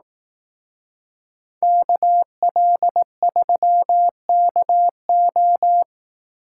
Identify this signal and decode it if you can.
Morse code, heard as a keyed tone, about 18 wpm, 700 Hz. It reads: KL3KO
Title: Speed Racing - USA 2x2 Call Signs 15wpm